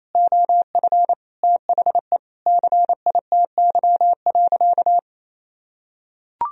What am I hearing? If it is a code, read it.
Morse code, heard as a keyed tone, about 28 wpm, 700 Hz. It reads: OF THE CITY.
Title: Sentences from Top 1000 Words - Review Part 2 - 28wpm